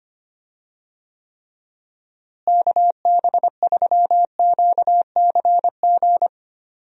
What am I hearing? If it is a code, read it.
Morse code, heard as a keyed tone, about 25 wpm, 700 Hz. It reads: KB3QCG
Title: USA Call Signs - 6 Characters - Repeated 3x - 25wpm